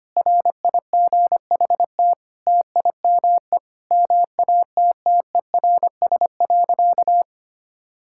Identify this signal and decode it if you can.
Morse code, heard as a keyed tone, about 25 wpm, 700 Hz. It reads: RIGHT TIME MATTERS.
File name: practice / 25wpm / Sentences with 9-Word Phrases - 25wpm